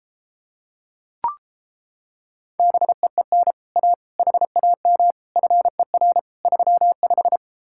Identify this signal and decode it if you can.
Morse code, heard as a keyed tone, about 33 wpm, 700 Hz. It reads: BEEN A HAM FER 35